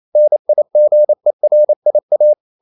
Morse code, heard as a keyed tone, about 28 wpm, 600 Hz. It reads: NIGERIA
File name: CW / Practice / NationalCapitals28